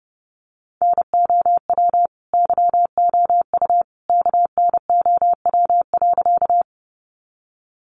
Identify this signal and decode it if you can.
Morse code, heard as a keyed tone, about 30 wpm, 700 Hz. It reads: NOW YOU KNOW.
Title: Sentences from Top 100 Words 30wpm